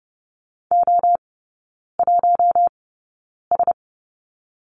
Morse code, heard as a keyed tone, about 30 wpm, spaced as for 9 wpm, 700 Hz. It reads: O1S
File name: Lesson 16e - Letter c - Three Characters 30wpm